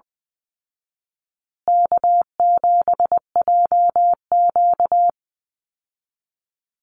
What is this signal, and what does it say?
Morse code, heard as a keyed tone, about 19 wpm, 700 Hz. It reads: K7JQ
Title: Easy Call Signs Repeated 3x - 20wpm